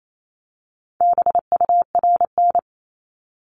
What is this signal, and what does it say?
Morse code, heard as a keyed tone, about 28 wpm, 700 Hz. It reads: BURN
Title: Multi-letter Phonemes Repeated 3x 28wpm